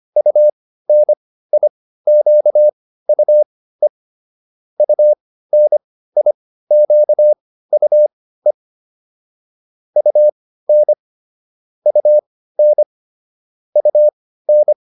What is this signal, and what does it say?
Morse code, heard as a keyed tone, about 25 wpm, spaced as for 15 wpm, 600 Hz. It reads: UNIQUE UNIQUE UN UN UN